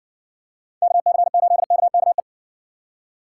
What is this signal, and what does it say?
Morse code, heard as a keyed tone, about 56 wpm, 700 Hz. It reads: KC8KKE